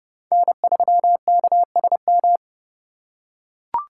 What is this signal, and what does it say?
Morse code, heard as a keyed tone, about 28 wpm, 700 Hz. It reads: N3KSM